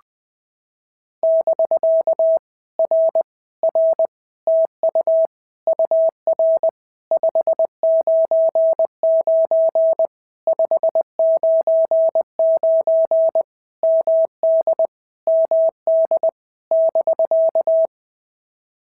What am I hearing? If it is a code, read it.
Morse code, heard as a keyed tone, about 20 wpm, 650 Hz. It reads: <BK> R R TU UR 599 599 MD MD <BK>
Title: POTA QSO Practice - 20wpm